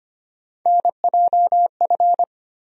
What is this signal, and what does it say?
Morse code, heard as a keyed tone, about 25 wpm, 700 Hz. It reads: NJF
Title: USA Call Sign Postfixes - 3 Letters 25wpm